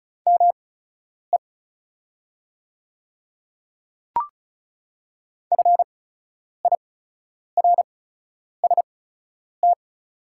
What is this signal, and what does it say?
Morse code, heard as a keyed tone, about 36 wpm, spaced as for 10 wpm, 690 Hz. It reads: ME FIRST
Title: 8x Character Spacing - Top 200 Words - 35wpm